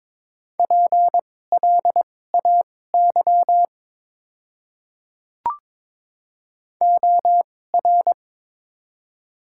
Morse code, heard as a keyed tone, about 22 wpm, spaced as for 16 wpm, 700 Hz. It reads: PLAY OR
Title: 2x Character Spacing - Top 300 Words - 22wpm